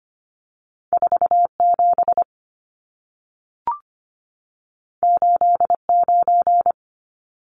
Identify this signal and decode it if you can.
Morse code, heard as a keyed tone, about 25 wpm, 700 Hz. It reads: 47 89
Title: Numbers - 2-Digit 25wpm